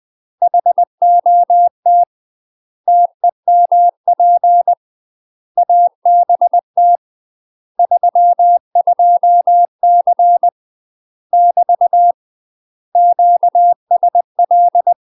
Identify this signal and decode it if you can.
Morse code, heard as a keyed tone, about 20 wpm, 700 Hz. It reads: HOT TEMP ABT 32C <BT> QSL
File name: UK MASTER QSOs TRAINING / 20WPM / UK MASTER QSOs 0048